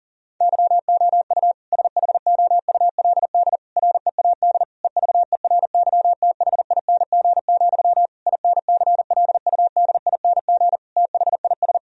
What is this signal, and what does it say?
Morse code, heard as a keyed tone, about 40 wpm, 700 Hz. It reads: YOU SHOULD READ EVERYTHING, INCLUDING THIS
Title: Sentences from Top 1000 Words - Review Part 4 - 40wpm